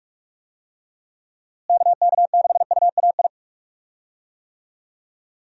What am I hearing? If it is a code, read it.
Morse code, heard as a keyed tone, about 46 wpm, 700 Hz. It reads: KK6UAI
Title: USA Call Signs - 6 Characters - Repeated 3x - 45wpm